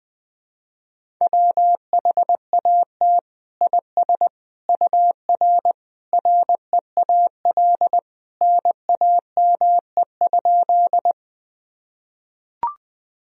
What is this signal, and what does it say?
Morse code, heard as a keyed tone, about 20 wpm, 700 Hz. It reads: WHAT IS UR REAL NAME?
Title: Sentences from Top 300 Words and Abbreviations 20wpm